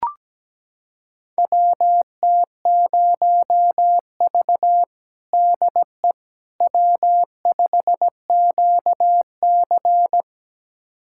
Morse code, heard as a keyed tone, about 17 wpm, 700 Hz. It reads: WT0V DE W5QC
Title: USA Callsign DE Callsign - 4 Characters - 17wpm